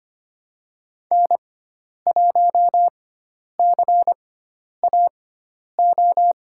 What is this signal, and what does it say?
Morse code, heard as a keyed tone, about 25 wpm, spaced as for 10 wpm, 700 Hz. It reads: N1CAO
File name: Lesson 16g - Letter c - Call Signs 25wpm